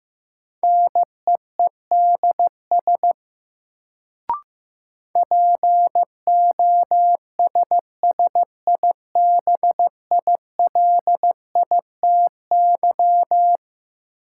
Morse code, heard as a keyed tone, about 15 wpm, 700 Hz. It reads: NEEDS POSSIBILITY
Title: Sets of 3 Words - New Word Drill - 15wpm